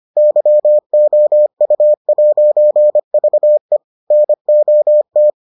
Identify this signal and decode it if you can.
Morse code, heard as a keyed tone, about 25 wpm, 600 Hz. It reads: YOU'VE NOT